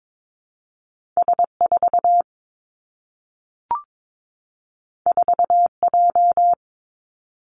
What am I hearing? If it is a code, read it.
Morse code, heard as a keyed tone, about 22 wpm, 700 Hz. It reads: S 4 4 J
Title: Two Challenging Letter-Number Combinations 22wpm